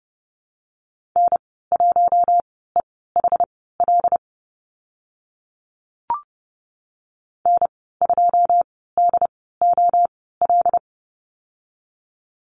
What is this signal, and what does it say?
Morse code, heard as a keyed tone, about 30 wpm, spaced as for 17 wpm, 700 Hz. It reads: N1EHL N2DOL
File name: Lesson 15g - Number 5 - Call Signs 30wpm